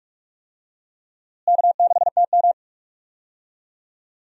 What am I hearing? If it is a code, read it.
Morse code, heard as a keyed tone, about 46 wpm, 700 Hz. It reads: K6TM
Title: USA Call Signs - 4 Characters - Repeated 2x - 45wpm